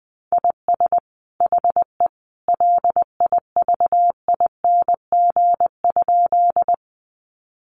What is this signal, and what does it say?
Morse code, heard as a keyed tone, about 20 wpm, 700 Hz. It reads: IS HE LIVING?
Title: Sentences from Top 1100 Words 20wpm